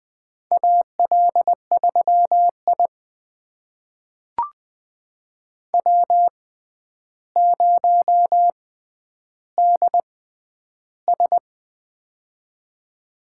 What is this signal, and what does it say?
Morse code, heard as a keyed tone, about 20 wpm, spaced as for 10 wpm, 700 Hz. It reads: AL3I W0DS
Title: USA Call Signs - 6x Character Spacing - 20wpm